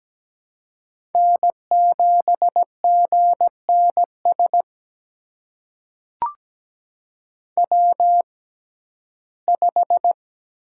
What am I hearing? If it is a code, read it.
Morse code, heard as a keyed tone, about 17 wpm, spaced as for 12 wpm, 700 Hz. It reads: N7GNS W5
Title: USA Call Signs - 6x Character Spacing - 17wpm